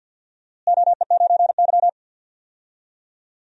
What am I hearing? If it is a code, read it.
Morse code, heard as a keyed tone, about 48 wpm, 700 Hz. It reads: OE9Y
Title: DXCC-100 Call Signs - 4 Characters - Repeated 2x - 50wpm